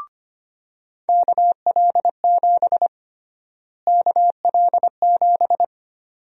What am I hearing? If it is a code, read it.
Morse code, heard as a keyed tone, about 25 wpm, 700 Hz. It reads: KL7 KL7
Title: USA Call Sign Prefixes - Repeated 2x - 25wpm